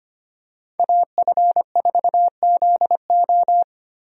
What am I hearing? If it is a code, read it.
Morse code, heard as a keyed tone, about 25 wpm, 700 Hz. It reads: AF4ZO
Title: USA Call Signs - 5 Characters - Repeated 3x - 25wpm